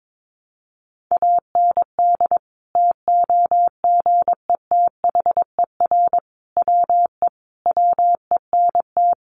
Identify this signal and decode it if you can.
Morse code, heard as a keyed tone, about 22 wpm, 700 Hz. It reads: AND TOGETHER WE WENT
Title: Sentences from Top 600 Words 22wpm